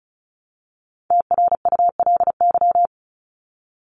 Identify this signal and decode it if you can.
Morse code, heard as a keyed tone, about 34 wpm, 700 Hz. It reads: TRULY
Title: LY Suffix - Top 100 Words with Base Words 35wpm